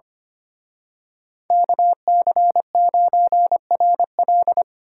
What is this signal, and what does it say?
Morse code, heard as a keyed tone, about 29 wpm, 700 Hz. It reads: KC9RL
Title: Speed Racing - USA 2x2 Call Signs 25wpm